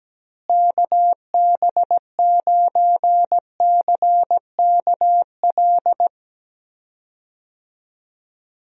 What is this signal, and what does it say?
Morse code, heard as a keyed tone, about 17 wpm, 700 Hz. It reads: KB9CKL
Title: USA Call Signs - 6 Characters - Repeated 2x - 17wpm